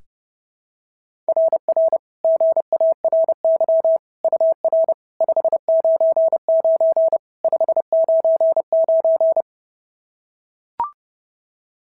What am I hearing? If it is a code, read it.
Morse code, heard as a keyed tone, about 28 wpm, 650 Hz. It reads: RR GARY UR 599 599
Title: SOTA QSO Practice 30wpm